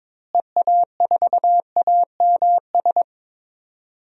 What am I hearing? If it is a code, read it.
Morse code, heard as a keyed tone, about 22 wpm, 700 Hz. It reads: EA4AMS
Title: DXCC-100 Call Signs - 6 Characters - Repeated 3x - 22wpm